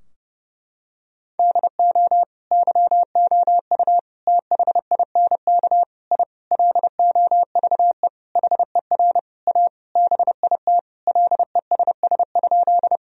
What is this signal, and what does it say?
Morse code, heard as a keyed tone, about 30 wpm, 700 Hz. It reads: DO YOU THINK I LOVE HER A BIT LESS?
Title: Sentences from Top 1000 Words - Review Part 2 - 30wpm